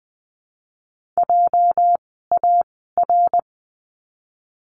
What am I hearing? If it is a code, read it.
Morse code, heard as a keyed tone, about 20 wpm, 700 Hz. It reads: J A R
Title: 206 Lesson 33f - Letter j - Words 20wpm